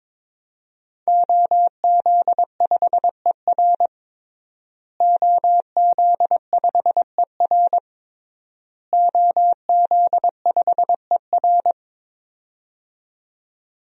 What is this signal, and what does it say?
Morse code, heard as a keyed tone, about 22 wpm, 700 Hz. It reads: OZ5ER OZ5ER OZ5ER
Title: DXCC-100 Call Signs - 5 Characters - Repeated 3x - 22wpm